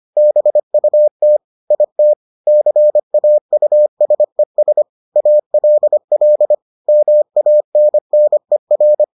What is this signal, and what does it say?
Morse code, heard as a keyed tone, about 25 wpm, 600 Hz. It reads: BUT IT CAUSES ALL MANNER